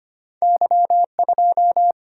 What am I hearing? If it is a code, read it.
Morse code, heard as a keyed tone, about 25 wpm, 700 Hz. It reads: Y2
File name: Two Letter-Number Combinations - 25wpm